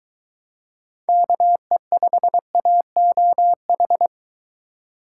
Morse code, heard as a keyed tone, about 22 wpm, 700 Hz. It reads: KE5AOH